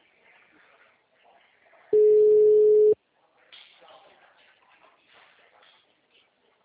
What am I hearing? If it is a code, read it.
Ringback tone (Europe, 425 Hz), with speech.